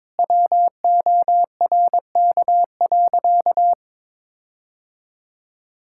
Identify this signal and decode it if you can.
Morse code, heard as a keyed tone, about 22 wpm, 700 Hz. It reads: WORK.
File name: Sentences with 7-Word Phrases - 22wpm